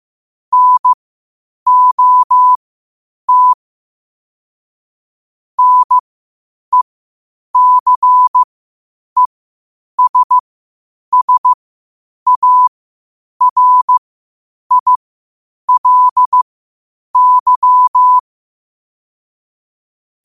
Morse code, heard as a keyed tone, about 15 wpm, spaced as for 8 wpm, 1000 Hz. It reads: NOT NECESSARILY